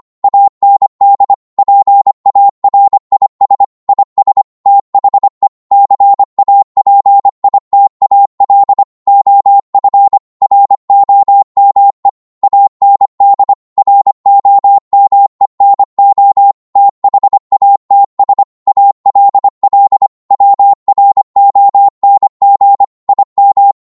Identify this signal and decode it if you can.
Morse code, heard as a keyed tone, about 25 wpm, 800 Hz. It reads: AND PARIS IS THE CAPITAL OF ROME AND ROME NO THATS ALL WRONG IM